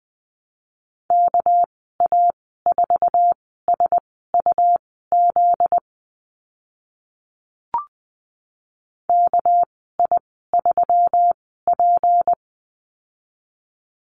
Morse code, heard as a keyed tone, about 20 wpm, spaced as for 15 wpm, 700 Hz. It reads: KA4SUZ KI3P
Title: 245 Lesson 39g - Letter z - Call Signs 20wpm